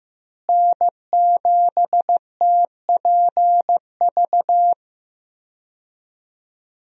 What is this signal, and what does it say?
Morse code, heard as a keyed tone, about 15 wpm, 700 Hz. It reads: N7TPV
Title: USA Call Signs - 5 Characters - Repeated 2x - 15wpm